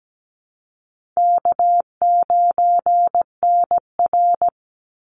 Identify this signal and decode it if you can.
Morse code, heard as a keyed tone, about 17 wpm, 700 Hz. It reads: K9NR
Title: Easy Call Signs - 17wpm